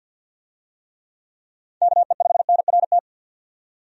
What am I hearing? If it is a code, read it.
Morse code, heard as a keyed tone, about 50 wpm, 700 Hz. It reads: KE5NRT
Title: USA Call Signs - 6 Characters - 50wpm